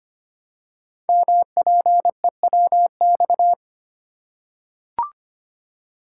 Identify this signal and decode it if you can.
Morse code, heard as a keyed tone, about 25 wpm, 700 Hz. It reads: MPEWX
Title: Code Groups - 5 Letters - 25wpm